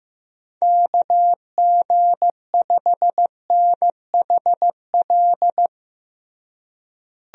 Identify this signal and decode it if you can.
Morse code, heard as a keyed tone, about 16 wpm, 700 Hz. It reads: KG5NHL
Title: USA Call Signs - 6 Characters - Repeated 2x - 15wpm